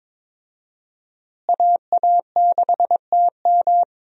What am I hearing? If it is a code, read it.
Morse code, heard as a keyed tone, about 22 wpm, 700 Hz. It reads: AA6TM
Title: USA Call Signs - 5 Characters - Repeated 3x - 22wpm